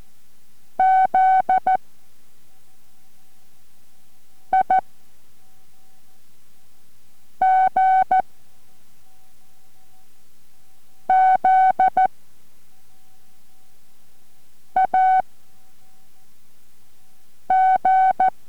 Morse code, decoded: Z I G Z A G